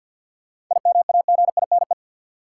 Morse code, heard as a keyed tone, about 48 wpm, 700 Hz. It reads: IMAGINE
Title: Sentences from Top 1400 Words - New Word Drill Alt - 50wpm